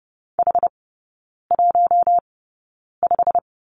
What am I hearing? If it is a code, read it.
Morse code, heard as a keyed tone, about 30 wpm, spaced as for 9 wpm, 700 Hz. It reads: H15
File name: Lesson 15e - Number 5 - Three Characters 30wpm